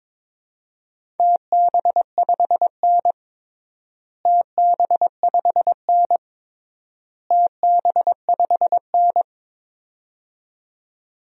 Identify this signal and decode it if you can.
Morse code, heard as a keyed tone, about 22 wpm, 700 Hz. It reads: TB5N TB5N TB5N
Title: DXCC-100 Call Signs - 4 Characters - Repeated 3x - 22wpm